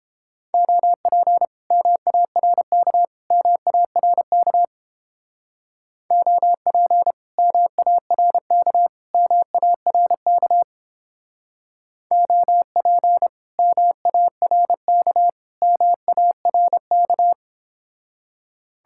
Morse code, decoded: OP MARK MARK OP MARK MARK OP MARK MARK

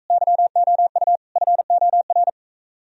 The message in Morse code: YOU FOR

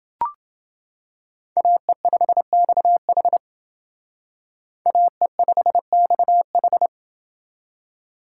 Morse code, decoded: AE5XH AE5XH

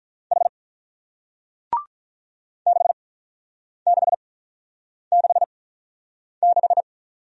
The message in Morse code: H 6666

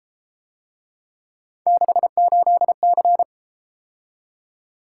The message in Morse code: 68C

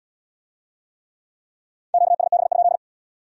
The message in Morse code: QSB?

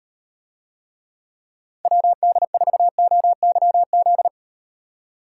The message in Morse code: WD4OYZ

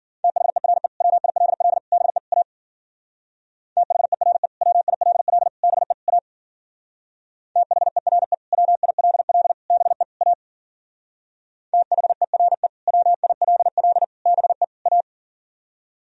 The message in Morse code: THERE WILL BE A THERE WILL BE A THERE WILL BE A THERE WILL BE A

